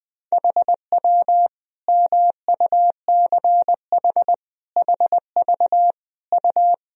Morse code, heard as a keyed tone, about 21 wpm, 700 Hz. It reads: HW MUCH HV U
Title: Sentences from Top 300 Words and Abbreviations 20wpm